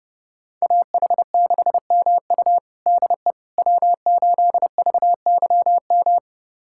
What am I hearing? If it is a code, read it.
Morse code, heard as a keyed tone, about 32 wpm, 700 Hz. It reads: AH6MU DE W8VYM
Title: USA Callsign DE Callsign - 5 Characters - 30wpm